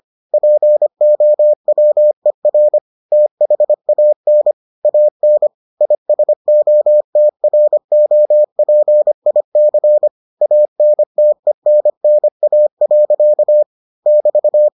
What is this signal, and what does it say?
Morse code, heard as a keyed tone, about 25 wpm, 600 Hz. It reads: POWER THAN AN ISOTROPIC ANTENNA. <BT>